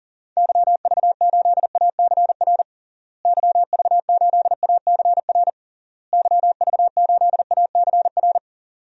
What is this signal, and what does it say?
Morse code, heard as a keyed tone, about 40 wpm, 700 Hz. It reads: YV8ACR YV8ACR YV8ACR
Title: DXCC-100 Call Signs - 6 Characters - Repeated 3x - 40wpm